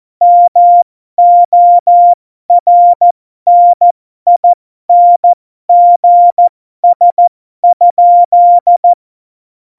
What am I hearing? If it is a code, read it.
Morse code, heard as a keyed tone, about 14 wpm, spaced as for 12 wpm, 700 Hz. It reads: MORNINGS?
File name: Morsekurs / Witze